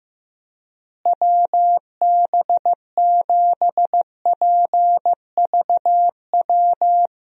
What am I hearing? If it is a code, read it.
Morse code, heard as a keyed tone, about 15 wpm, 700 Hz. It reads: WB7PVW
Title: USA Call Signs - 6 Characters - Repeated 2x - 15wpm